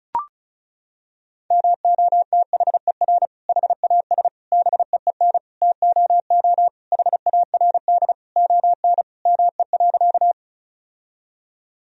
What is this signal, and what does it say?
Morse code, heard as a keyed tone, about 36 wpm, 700 Hz. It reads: MOTHER HAS BEEN TOO HARD ON ME.
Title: Sentences from Top 1000 Words - Review Part 1 - 35wpm